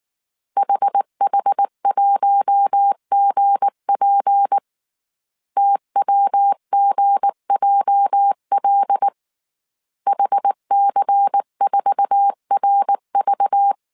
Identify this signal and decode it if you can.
Morse code, heard as a keyed tone, about 19 wpm, 800 Hz. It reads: HH1GP TWGJL HC4RV
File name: CW / DET / 1 9 1500C S A 18wpm 2459